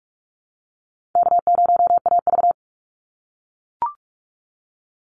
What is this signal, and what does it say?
Morse code, heard as a keyed tone, about 44 wpm, 700 Hz. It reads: K0AV
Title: Call Signs - 45wpm